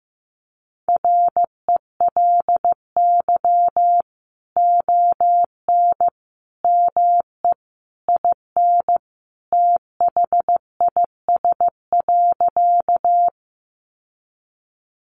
Morse code, heard as a keyed tone, about 15 wpm, 700 Hz. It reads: RELY ON ME IN THIS.
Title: Sentences from Top 1200 Words 15wpm